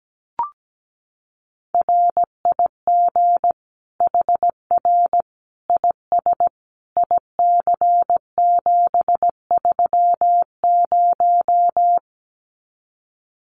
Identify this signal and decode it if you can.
Morse code, heard as a keyed tone, about 17 wpm, 700 Hz. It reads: RIG HR IS IC730
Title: Common QSO Phrases 17wpm